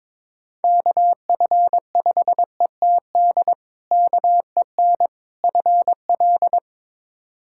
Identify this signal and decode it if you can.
Morse code, heard as a keyed tone, about 22 wpm, 700 Hz. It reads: KF5ETD KEN FL
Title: K1USN SST QSO Practice - 22wpm